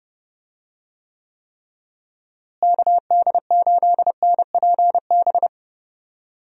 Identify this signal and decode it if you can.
Morse code, heard as a keyed tone, about 30 wpm, 700 Hz. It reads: KD8NPB
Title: USA Call Signs - 6 Characters - Repeated 3x - 30wpm